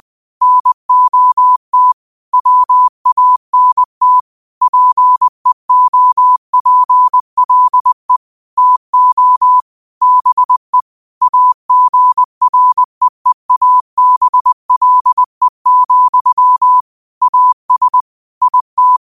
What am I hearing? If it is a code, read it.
Morse code, heard as a keyed tone, about 20 wpm, 1000 Hz. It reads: NOT WANT PEOPLE TO BE AGREEABLE, AS IT
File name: qotdfiles / 20wpm